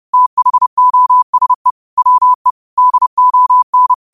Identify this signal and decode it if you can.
Morse code, heard as a keyed tone, about 30 wpm, 1000 Hz. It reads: THOSE WE DON
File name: qotdfiles / 30wpm